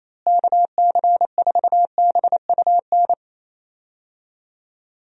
Morse code, heard as a keyed tone, about 28 wpm, 700 Hz. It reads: KC4BUN
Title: USA Call Signs - 6 Characters - Repeated 2x - 28wpm